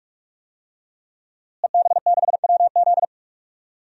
Morse code, heard as a keyed tone, about 44 wpm, 700 Hz. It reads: EB6WZ
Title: DXCC-100 Call Signs - 5 Characters - Repeated 3x - 45wpm